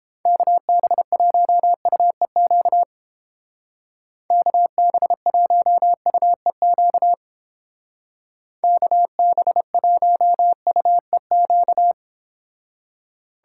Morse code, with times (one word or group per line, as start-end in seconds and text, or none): 0.3-2.8 KB1UEQ
4.3-7.1 KB1UEQ
8.6-11.9 KB1UEQ